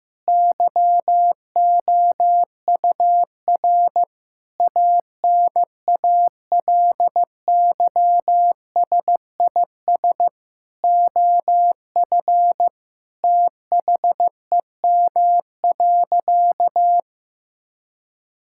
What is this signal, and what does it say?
Morse code, heard as a keyed tone, about 14 wpm, 700 Hz. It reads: YOUR ANALYSIS OF THEM.
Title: Sentences from Top 1000 Words - Review Part 5 - 15wpm